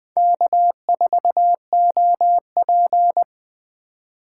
Morse code, heard as a keyed tone, about 23 wpm, 700 Hz. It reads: K4OP
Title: Speed Racing - USA Call Signs 20wpm